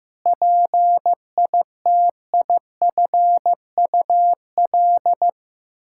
Morse code, heard as a keyed tone, about 15 wpm, 700 Hz. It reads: PITIFUL
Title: Pangram Challenge v2 - 15wpm